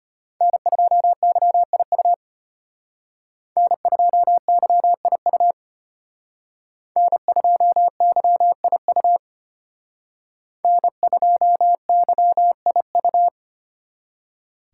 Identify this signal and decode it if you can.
Morse code, heard as a keyed tone, about 29 wpm, 700 Hz. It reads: N2YIU N2YIU N2YIU N2YIU
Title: Speed Racing - USA 1x3 Call Signs 25wpm